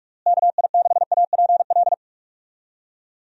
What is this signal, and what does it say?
Morse code, heard as a keyed tone, about 46 wpm, 700 Hz. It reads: KI6APL
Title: USA Call Signs - 6 Characters - Repeated 3x - 45wpm